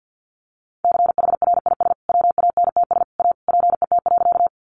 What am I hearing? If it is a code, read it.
Morse code, heard as a keyed tone, about 48 wpm, 700 Hz. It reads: CHRIS WANTS A PET.
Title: Additional Easy Phrases 50wpm